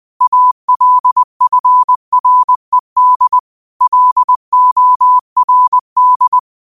Morse code, heard as a keyed tone, about 20 wpm, 1000 Hz. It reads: ALFRED LORD
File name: qotdfiles / 20wpm